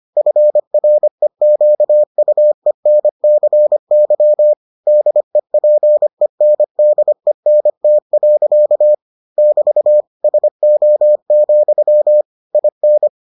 Morse code, heard as a keyed tone, about 25 wpm, 600 Hz. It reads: FREQUENCY DEPENDENT. <BT> SO, IN